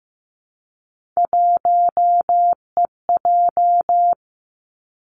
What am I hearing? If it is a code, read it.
Morse code, heard as a keyed tone, about 15 wpm, 700 Hz. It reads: 1EJ
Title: Three Challenging Letter-Number Combinations 15wpm